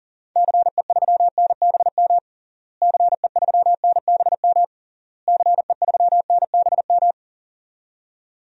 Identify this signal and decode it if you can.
Morse code, heard as a keyed tone, about 40 wpm, 700 Hz. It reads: CE3NBM CE3NBM CE3NBM
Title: DXCC-100 Call Signs - 6 Characters - Repeated 3x - 40wpm